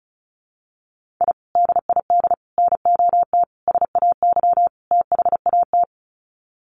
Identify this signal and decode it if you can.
Morse code, heard as a keyed tone, about 34 wpm, 700 Hz. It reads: I DID NOT SAY THAT
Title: Sets of 5 Words Encore2 35wpm